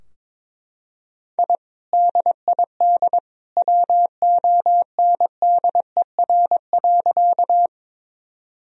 Morse code, heard as a keyed tone, about 22 wpm, 700 Hz. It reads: I DID WONDER.
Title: Sentences from Top 1000 Words - Review Part 3 - 22wpm